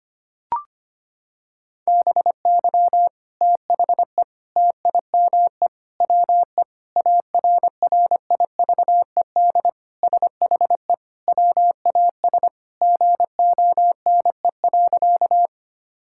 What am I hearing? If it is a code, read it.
Morse code, heard as a keyed tone, about 25 wpm, 700 Hz. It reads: BY THE TIME WE ARRIVED SHE WAS GONE.
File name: practice / 25wpm / Sentences with 3-Word Phrases - 25wpm